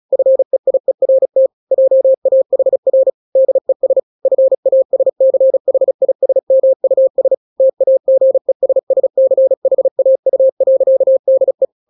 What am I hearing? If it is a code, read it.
Morse code, heard as a keyed tone, about 36 wpm, 530 Hz. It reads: FEIERT JAHR DES FASCHISMUS TAGESSCHAU.DE
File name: daily / 35wpm